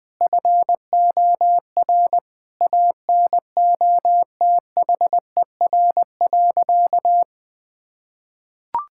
Morse code, heard as a keyed tone, about 20 wpm, 700 Hz. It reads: FOR ANOTHER.
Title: Sentences from Top 1000 Words - Review Part 6 - 20wpm